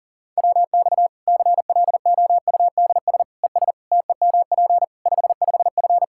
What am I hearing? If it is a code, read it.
Morse code, heard as a keyed tone, about 40 wpm, 700 Hz. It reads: WX CLOUDS ES TEMP 55F
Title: K1USN SST QSO Practice 40wpm